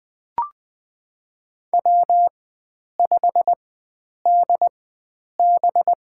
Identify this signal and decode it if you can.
Morse code, heard as a keyed tone, about 20 wpm, spaced as for 9 wpm, 700 Hz. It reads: W5DB
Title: USA Call Signs - 4 Characters - 4x Character Spacing - 20wpm